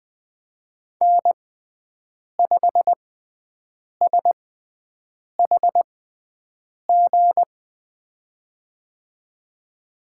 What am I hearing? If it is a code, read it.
Morse code, heard as a keyed tone, about 20 wpm, spaced as for 7 wpm, 700 Hz. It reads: N5SHG
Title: USA Call Signs - 6x Character Spacing - 20wpm